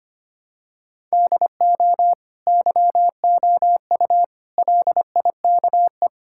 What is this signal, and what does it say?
Morse code, heard as a keyed tone, about 25 wpm, 700 Hz. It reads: DO YOU LIKE
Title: Sentences from Top 1000 Words - Review Part 5 - 25wpm